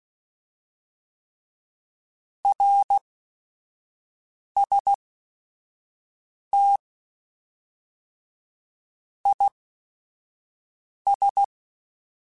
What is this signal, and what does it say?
Morse code, heard as a keyed tone, about 16 wpm, spaced as for 5 wpm, 800 Hz. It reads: RST IS